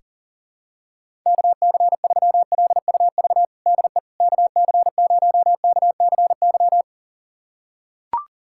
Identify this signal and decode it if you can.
Morse code, heard as a keyed tone, about 40 wpm, 700 Hz. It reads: KC3LUV DE KC0KCY
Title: USA Callsign DE Callsign 40wpm